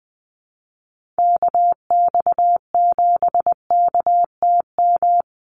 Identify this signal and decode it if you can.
Morse code, heard as a keyed tone, about 20 wpm, 700 Hz. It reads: KX7KTM
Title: USA Call Signs 20wpm